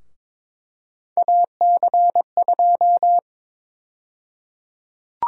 Morse code, heard as a keyed tone, about 21 wpm, 700 Hz. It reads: AC2
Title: USA Call Sign Prefixes - Repeated 2x - 22wpm